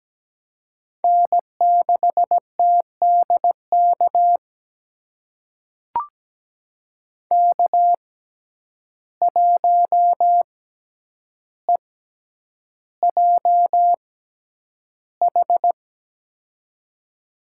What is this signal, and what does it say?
Morse code, heard as a keyed tone, about 17 wpm, spaced as for 9 wpm, 700 Hz. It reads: N6TDK K1EJH
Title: USA Call Signs - 5 Characters - 6x Character Spacing - 17wpm